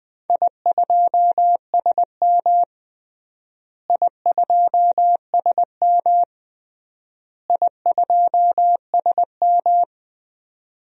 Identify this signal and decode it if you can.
Morse code, heard as a keyed tone, about 20 wpm, 700 Hz. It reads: I2SM I2SM I2SM
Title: DXCC-100 Call Signs - 4 Characters - Repeated 3x - 20wpm